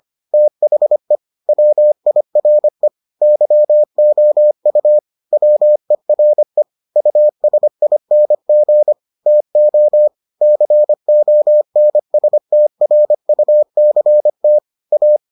Morse code, heard as a keyed tone, about 25 wpm, 600 Hz. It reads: THE WIRE YOU WERE USING TO CONSTRUCT A